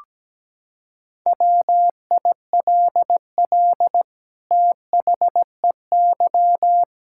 Morse code, heard as a keyed tone, about 17 wpm, 700 Hz. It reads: WILL THEY
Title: Sentences from Top 1000 Words - Review Part 5 - 17wpm